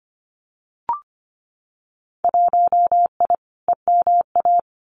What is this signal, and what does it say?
Morse code, heard as a keyed tone, about 25 wpm, 700 Hz. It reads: 1I EMA
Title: Winter Field Day Exchanges Encore 25wpm